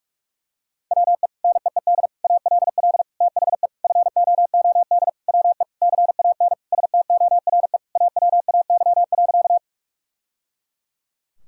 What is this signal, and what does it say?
Morse code, heard as a keyed tone, about 46 wpm, 700 Hz. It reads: WE NEED ALL THE FOOD WE CAN STORE AWAY.
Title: Sentences from Top 1000 Words - Review Part 5 - 45wpm